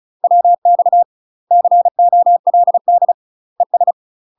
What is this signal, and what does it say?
Morse code, heard as a keyed tone, about 34 wpm, 700 Hz. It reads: WX COLD ES
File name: UK MASTER QSOs TRAINING/35WPM